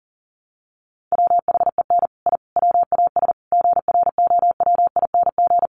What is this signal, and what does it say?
Morse code, heard as a keyed tone, about 40 wpm, 700 Hz. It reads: WHEN I WAS GROWING
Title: Sets of 5 Words Encore 40wpm